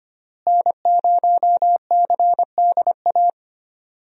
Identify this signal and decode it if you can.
Morse code, heard as a keyed tone, about 25 wpm, 700 Hz. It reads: N0CDA
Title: USA Call Signs - 5 Characters - Repeated 3x - 25wpm